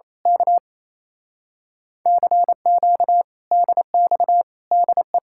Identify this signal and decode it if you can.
Morse code, heard as a keyed tone, about 26 wpm, 700 Hz. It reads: K CQ DX DE
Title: Speed-Racing Common QSO Phrases Encore - 28wpm